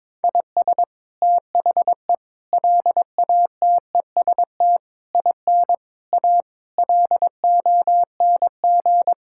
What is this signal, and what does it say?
Morse code, heard as a keyed tone, about 22 wpm, 700 Hz. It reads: IS THE LATEST IN A LONG
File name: practice / 22wpm / Sets of 8 Words - 22wpm